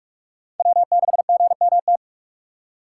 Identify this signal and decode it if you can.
Morse code, heard as a keyed tone, about 46 wpm, 700 Hz. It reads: W6GMT
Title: USA Call Signs - 2x Character Spacing - 45wpm